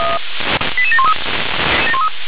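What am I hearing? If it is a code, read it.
Five-tone selective call (ZVEI), heard as quick runs of five tones, tones of about 70 ms. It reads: 88713
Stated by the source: ZVEI five-tone selective call received with an SDR.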